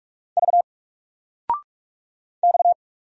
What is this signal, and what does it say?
Morse code, heard as a keyed tone, about 44 wpm, 700 Hz. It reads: V X
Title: Single Letter-Number 45wpm